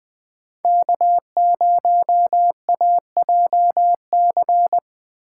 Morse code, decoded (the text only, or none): K0AJC